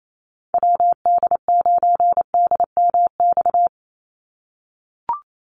Morse code, heard as a keyed tone, about 28 wpm, 700 Hz. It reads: WD9DMX
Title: USA Call Signs 28wpm